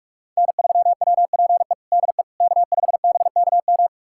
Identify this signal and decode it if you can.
Morse code, heard as a keyed tone, about 46 wpm, 700 Hz. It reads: N3WPE DE K5BKM
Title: USA Callsign DE Callsign - 5 Characters - 45wpm